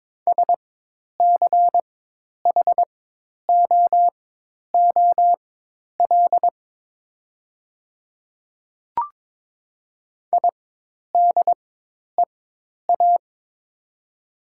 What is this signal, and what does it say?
Morse code, heard as a keyed tone, about 22 wpm, spaced as for 10 wpm, 700 Hz. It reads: SCHOOL IDEA
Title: 4x Character Spacing - Top 300 Words - 22wpm